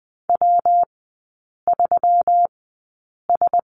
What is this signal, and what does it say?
Morse code, heard as a keyed tone, about 20 wpm, spaced as for 8 wpm, 700 Hz. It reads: W3S
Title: 191 Lesson 31e - Letter v - Three Characters 20wpm